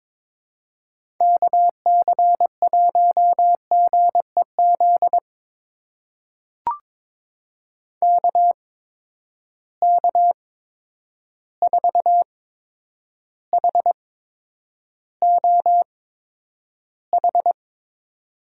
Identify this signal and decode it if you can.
Morse code, heard as a keyed tone, about 22 wpm, spaced as for 9 wpm, 700 Hz. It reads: KC1GEZ KK4HOH
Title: USA Call Signs - 6 Characters - 8x Character Spacing - 22wpm